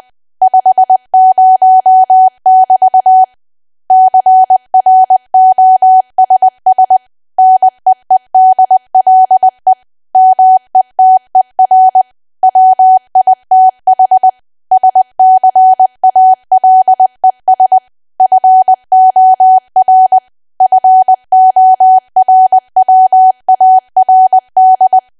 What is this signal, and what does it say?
Morse code, heard as a keyed tone, about 20 wpm, 750 Hz. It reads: 50<BT> CROSS NEEDLE METER WITH SCALES FOR FORWARD
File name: cwa / W1AW-Level3 / 20WPM